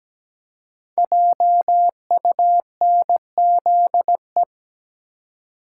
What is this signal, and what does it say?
Morse code, heard as a keyed tone, about 17 wpm, 700 Hz. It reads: JUNZE
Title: Code Groups - 5 Letters - 17wpm